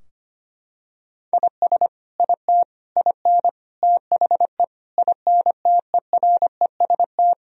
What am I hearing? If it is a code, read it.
Morse code, heard as a keyed tone, about 25 wpm, 700 Hz. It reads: IS IT IN THE INTEREST
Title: Sentences from Top 1000 Words - Review Part 4 - 25wpm